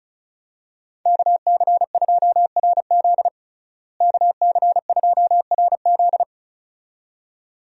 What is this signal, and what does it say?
Morse code, heard as a keyed tone, about 36 wpm, 700 Hz. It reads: KC2RZ KC2RZ
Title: USA Call Signs - 5 Characters - Repeated 2x - 35wpm